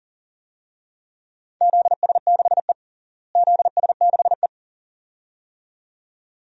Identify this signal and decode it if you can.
Morse code, heard as a keyed tone, about 37 wpm, 700 Hz. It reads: ZS6E ZS6E
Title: DXCC-100 Call Signs - 4 Characters - Repeated 2x - 40wpm